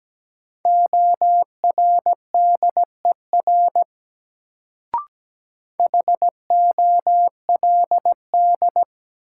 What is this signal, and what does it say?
Morse code, heard as a keyed tone, about 17 wpm, 700 Hz. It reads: ORDER HOLD
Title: Sentences from Top 400 Words - New Word Drill - 17wpm